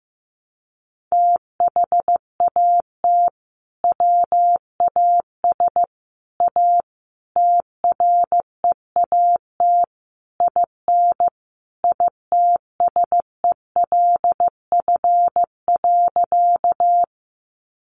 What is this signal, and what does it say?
Morse code, heard as a keyed tone, about 15 wpm, 700 Hz. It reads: THAT WAS A TREAT IN ITSELF.
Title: Sentences from Top 1300 Words 15wpm